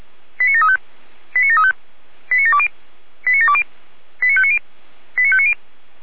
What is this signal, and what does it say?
Five-tone selective call (ZVEI), heard as quick runs of five tones, tones of about 70 ms. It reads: 89735 89735 89720 89720 89590 89590